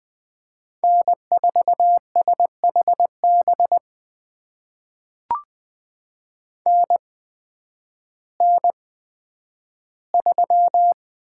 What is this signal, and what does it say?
Morse code, heard as a keyed tone, about 20 wpm, spaced as for 11 wpm, 700 Hz. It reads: N4SHB NN3
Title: USA Call Signs - 8x Character Spacing - 20wpm